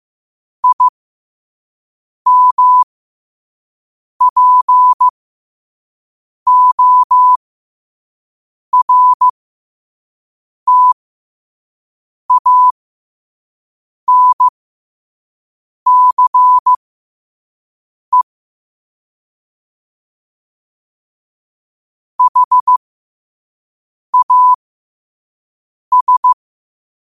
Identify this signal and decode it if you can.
Morse code, heard as a keyed tone, about 15 wpm, spaced as for 5 wpm, 1000 Hz. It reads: IMPORTANCE HAS
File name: qotdfiles / 5wpm